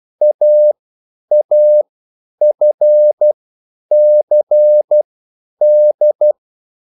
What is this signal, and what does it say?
Morse code, heard as a keyed tone, about 12 wpm, spaced as for 9 wpm, 600 Hz. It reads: AAFCD